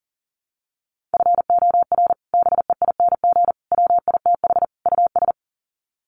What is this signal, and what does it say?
Morse code, heard as a keyed tone, about 37 wpm, 700 Hz. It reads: FOR BEING WITH US
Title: Sets of 4 Words 40wpm